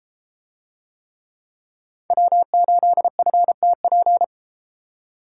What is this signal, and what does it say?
Morse code, heard as a keyed tone, about 32 wpm, 700 Hz. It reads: W8FTP